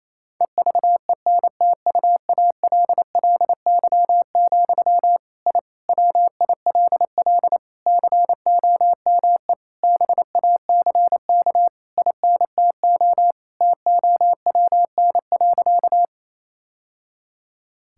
Morse code, heard as a keyed tone, about 28 wpm, 700 Hz. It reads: EVENTUALLY, I WILL COME BACK INTO TOWN.
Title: Sentences from Top 1000 Words - Review Part 6 - 28wpm